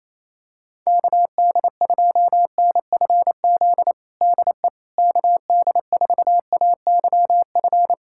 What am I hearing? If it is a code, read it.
Morse code, heard as a keyed tone, about 28 wpm, 700 Hz. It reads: KD2NFZ DE KD4AYF
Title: USA Callsign DE Callsign - 6 Characters - 28wpm